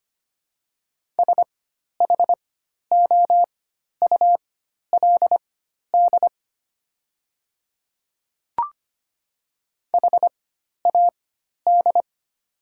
Morse code, heard as a keyed tone, about 25 wpm, spaced as for 12 wpm, 700 Hz. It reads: SHOULD HAD
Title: 4x Character Spacing - Top 500 Words - 25wpm